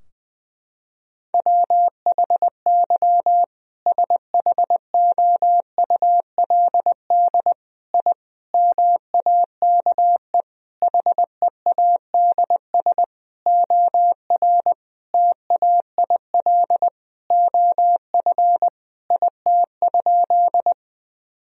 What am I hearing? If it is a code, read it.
Morse code, heard as a keyed tone, about 20 wpm, 700 Hz. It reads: WHY SHOULD I MAKE HEADS OR TAIL OF IT?
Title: Sentences from Top 1000 Words - Review Part 3 - 20wpm